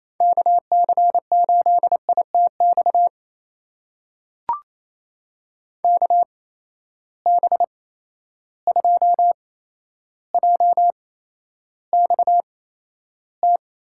Morse code, heard as a keyed tone, about 28 wpm, spaced as for 12 wpm, 700 Hz. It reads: KC8ITX KB2JXT
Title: USA Call Signs - 6 Characters - 8x Character Spacing - 28wpm